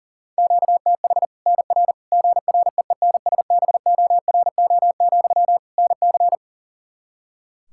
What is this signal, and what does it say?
Morse code, heard as a keyed tone, about 40 wpm, 700 Hz. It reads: QTH NR GREENSBORO, NC
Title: Practice QSO - Exchange 1 - Beginning - 40wpm